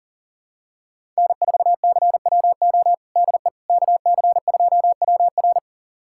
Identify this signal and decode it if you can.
Morse code, heard as a keyed tone, about 40 wpm, 700 Hz. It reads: N4CWO DE KC2WR
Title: USA Callsign DE Callsign - 5 Characters - 40wpm